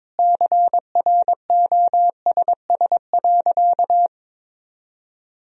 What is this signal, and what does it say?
Morse code, heard as a keyed tone, about 22 wpm, 700 Hz. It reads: CROSS.
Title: Sentences from Top 1000 Words - Review Part 1 - 22wpm